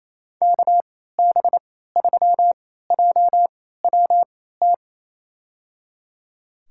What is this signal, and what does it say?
Morse code, heard as a keyed tone, about 28 wpm, spaced as for 16 wpm, 700 Hz. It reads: KB3JWT
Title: USA Call Signs - 6 Characters - 3x Character Spacing - 28wpm